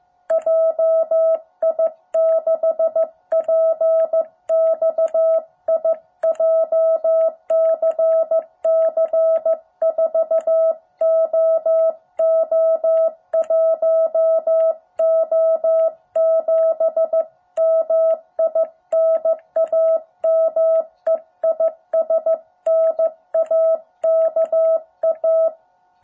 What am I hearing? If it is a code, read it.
Morse code, heard as a keyed tone, about 15 wpm, 650 Hz. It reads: JI6PXI JCC 4OO1O7 MI NAME IS NAKA
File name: jh8cbh / 2021cw